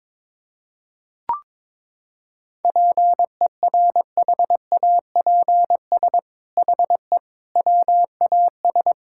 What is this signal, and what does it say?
Morse code, heard as a keyed tone, about 22 wpm, 700 Hz. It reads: PERHAPS HE WAS
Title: Sentences from Top 1600 Words 22wpm